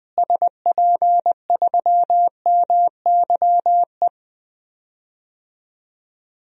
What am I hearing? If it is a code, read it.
Morse code, heard as a keyed tone, about 20 wpm, 700 Hz. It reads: SP3MYE
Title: DXCC-100 Call Signs - 6 Characters - Repeated 3x - 20wpm